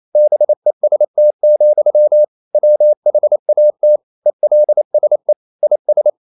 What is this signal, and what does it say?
Morse code, heard as a keyed tone, about 28 wpm, 600 Hz. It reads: BEST, WHAT ELSE IS
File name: CW / Practice / EnglishQuotations0128